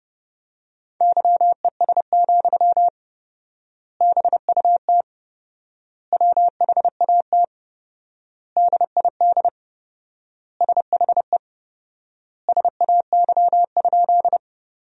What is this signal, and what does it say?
Morse code, heard as a keyed tone, about 29 wpm, 700 Hz. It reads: YES, BUT WHAT DID SHE SAY?
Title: 4x Word-Spacing - Sentences from Top 500 Words - 30wpm